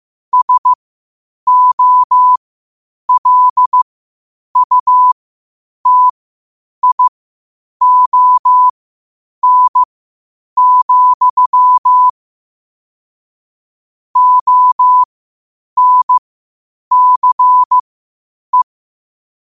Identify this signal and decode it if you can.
Morse code, heard as a keyed tone, about 15 wpm, spaced as for 8 wpm, 1000 Hz. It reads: SOLUTION, ONCE
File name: qotdfiles / 8wpm